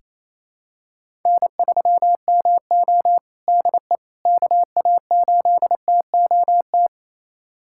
Morse code, heard as a keyed tone, about 28 wpm, 700 Hz. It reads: N3MO DE KA8TOT
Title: USA Callsign DE Callsign 28wpm